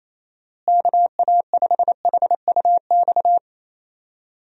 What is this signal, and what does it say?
Morse code, heard as a keyed tone, about 28 wpm, 700 Hz. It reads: KA5HUX
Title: USA Call Signs - 6 Characters - Repeated 2x - 28wpm